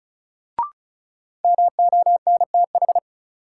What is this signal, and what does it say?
Morse code, heard as a keyed tone, about 36 wpm, 700 Hz. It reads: MONTH
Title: Sets of 5 Words Encore - New Word Drill - 35wpm